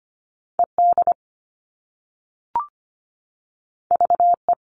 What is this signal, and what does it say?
Morse code, heard as a keyed tone, about 25 wpm, 700 Hz. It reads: E D V E
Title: All Cut Numbers - 2-Digit 25wpm